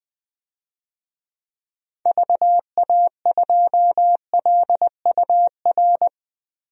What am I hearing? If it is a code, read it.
Morse code, heard as a keyed tone, about 20 wpm, 700 Hz. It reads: VA2LUR
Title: DXCC-100 Call Signs - 6 Characters - 20wpm